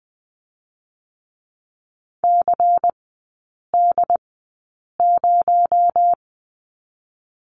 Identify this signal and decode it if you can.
Morse code, heard as a keyed tone, about 20 wpm, spaced as for 8 wpm, 700 Hz. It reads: CD0
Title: 218 Lesson 35e - Number 0 - Three Characters 20wpm